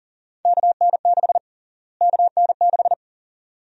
Morse code, heard as a keyed tone, about 40 wpm, 700 Hz. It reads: KN6 KN6
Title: USA Call Sign Prefixes - Repeated 2x - 40wpm